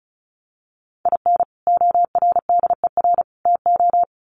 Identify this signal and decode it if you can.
Morse code, heard as a keyed tone, about 34 wpm, 700 Hz. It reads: IN ORDER TO
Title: Sets of 3 Words 35wpm